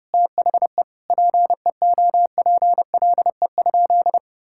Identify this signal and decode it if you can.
Morse code, heard as a keyed tone, about 32 wpm, 700 Hz. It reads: THE PEOPLE?
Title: Sentences from Top 1000 Words - Review Part 5 - 30wpm